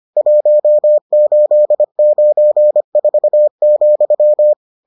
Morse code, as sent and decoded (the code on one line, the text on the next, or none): .---- ---.. ----. ....- --..--
1894,